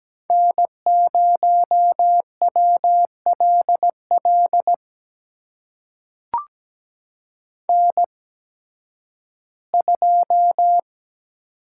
Morse code, heard as a keyed tone, about 17 wpm, spaced as for 11 wpm, 700 Hz. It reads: N0WLL N2
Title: USA Call Signs - 5 Characters - 8x Character Spacing - 17wpm